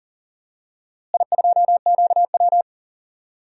Morse code, heard as a keyed tone, about 37 wpm, 700 Hz. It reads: I2QW
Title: DXCC-100 Call Signs - 4 Characters - Repeated 3x - 40wpm